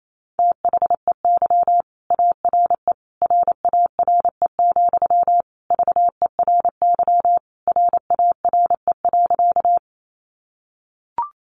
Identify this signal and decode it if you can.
Morse code, heard as a keyed tone, about 28 wpm, 700 Hz. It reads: THEY ARE RARE, VERY RARE.
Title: Sentences from Top 1100 Words 28wpm